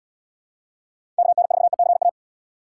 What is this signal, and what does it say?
Morse code, heard as a keyed tone, about 77 wpm, 700 Hz. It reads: KN4ECA